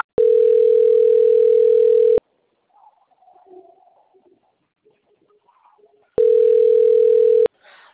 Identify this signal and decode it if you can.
Ringback tone (North America), 440 + 480 Hz.